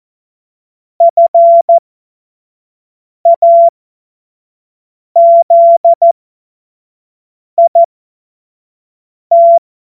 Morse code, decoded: FAZIT

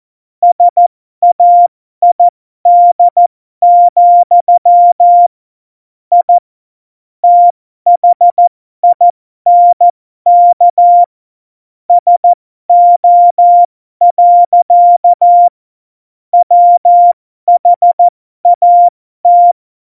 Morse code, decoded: SAID, I THINK SO. WHAT

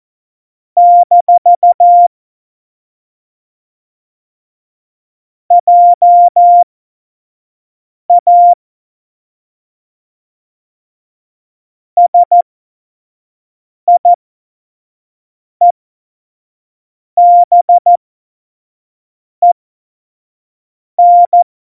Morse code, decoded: - JA SIEBEN

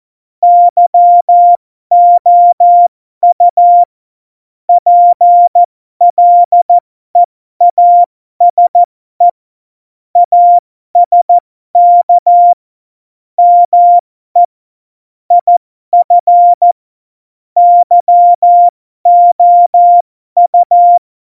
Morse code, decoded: YOU PLEASE ASK ME IF YOU

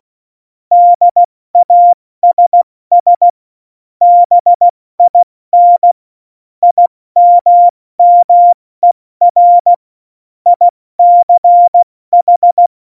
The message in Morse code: DASS BIN IMMER ICH

